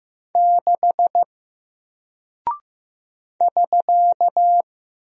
Morse code, heard as a keyed tone, about 15 wpm, 700 Hz. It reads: 6 <SK>
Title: Single Character (Letter:Number:Punctuation:Prosign) Alt 15wpm